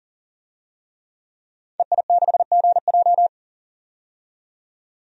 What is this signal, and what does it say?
Morse code, heard as a keyed tone, about 40 wpm, 700 Hz. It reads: EI6GJ
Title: DXCC-100 Call Signs - 5 Characters - 40wpm